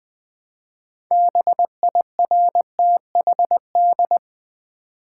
Morse code, heard as a keyed tone, about 20 wpm, 700 Hz. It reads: BIRTHD
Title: Build A Word - 3 Letters At A Time - 8-Letter Words - 20wpm